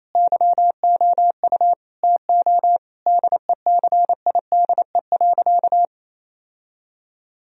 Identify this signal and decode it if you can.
Morse code, heard as a keyed tone, about 28 wpm, 700 Hz. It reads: YOU TO DECIDE.
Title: Sentences from Top 1000 Words - Review Part 3 - 28wpm